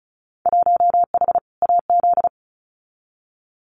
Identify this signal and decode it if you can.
Morse code, heard as a keyed tone, about 35 wpm, 700 Hz. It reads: 1H AZ
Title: Winter Field Day Exchanges 35wpm